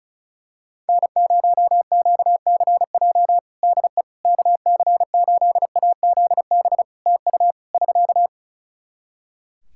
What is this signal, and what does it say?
Morse code, heard as a keyed tone, about 36 wpm, 700 Hz. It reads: N0QCJ DE KC8AZB TU <SK>
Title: Practice QSO - Exchange 4 - Ending - 35wpm